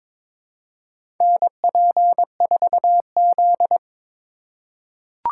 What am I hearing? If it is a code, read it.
Morse code, heard as a keyed tone, about 22 wpm, 700 Hz. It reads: NP4Z
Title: USA Call Signs - 4 Characters - 3x Character Spacing - 22wpm